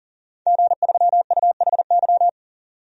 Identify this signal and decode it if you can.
Morse code, heard as a keyed tone, about 40 wpm, 700 Hz. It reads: G3UHY
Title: DXCC-100 Call Signs - 5 Characters - 40wpm